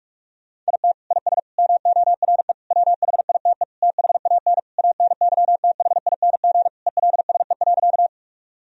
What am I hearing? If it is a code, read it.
Morse code, heard as a keyed tone, about 42 wpm, 700 Hz. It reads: IT IS MORE WHITE THAN ANYTHING ELSE.
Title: Sentences from Top 1000 Words - Review Part 1 - 45wpm